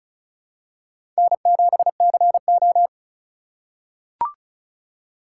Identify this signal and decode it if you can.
Morse code, heard as a keyed tone, about 36 wpm, 700 Hz. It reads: N7CO
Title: USA Call Signs - 4 Characters - 4x Character Spacing - 35wpm